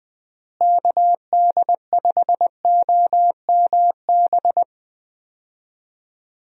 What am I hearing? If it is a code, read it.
Morse code, heard as a keyed tone, about 21 wpm, 700 Hz. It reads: KD5OMB
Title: USA Call Signs - 6 Characters - Repeated 3x - 20wpm